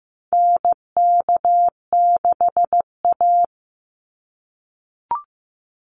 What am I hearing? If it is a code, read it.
Morse code, heard as a keyed tone, about 15 wpm, 700 Hz. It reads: NK6A
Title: NA SOTA Chaser Callsigns 15wpm